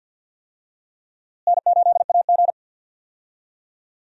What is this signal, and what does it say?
Morse code, heard as a keyed tone, about 47 wpm, 700 Hz. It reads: N8AG
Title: USA Call Signs - 4 Characters - Repeated 2x - 50wpm